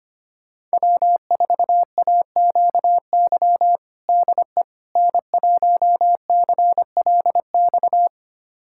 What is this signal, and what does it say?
Morse code, heard as a keyed tone, about 25 wpm, 700 Hz. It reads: W4AQY DE N1CLX
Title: USA Callsign DE Callsign - 5 Characters - 25wpm